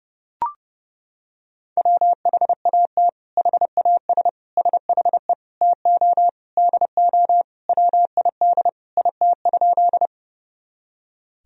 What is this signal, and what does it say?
Morse code, heard as a keyed tone, about 30 wpm, 700 Hz. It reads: WHAT HAS SHE TO DO WID IT?
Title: Sentences from Top 100 Words and Abbreviations 30wpm